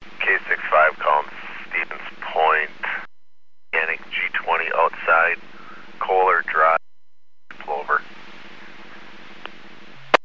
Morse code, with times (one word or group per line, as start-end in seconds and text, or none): none